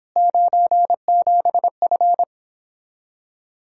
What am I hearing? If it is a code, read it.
Morse code, heard as a keyed tone, about 26 wpm, 700 Hz. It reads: 97F